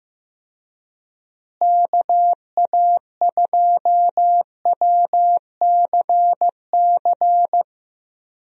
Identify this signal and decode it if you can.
Morse code, heard as a keyed tone, about 16 wpm, 700 Hz. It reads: KA2WCC
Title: USA Call Signs - 6 Characters - Repeated 3x - 15wpm